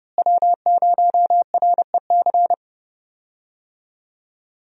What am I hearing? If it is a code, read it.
Morse code, heard as a keyed tone, about 31 wpm, 700 Hz. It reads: W0REC
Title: USA Call Signs - 5 Characters - Repeated 3x - 30wpm